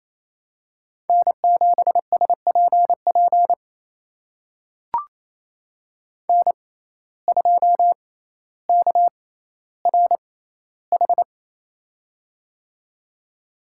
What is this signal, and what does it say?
Morse code, heard as a keyed tone, about 28 wpm, spaced as for 14 wpm, 700 Hz. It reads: N7SPP N2KRH
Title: USA Call Signs - 6x Character Spacing - 28wpm